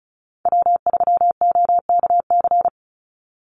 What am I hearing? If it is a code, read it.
Morse code, heard as a keyed tone, about 34 wpm, 700 Hz. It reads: W3OKC
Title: Call Signs Repeated 2x - 35wpm